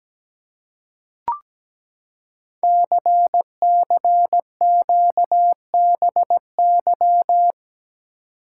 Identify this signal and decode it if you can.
Morse code, heard as a keyed tone, about 17 wpm, 700 Hz. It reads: CCQBY
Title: Code Groups - 5 Letters - 17wpm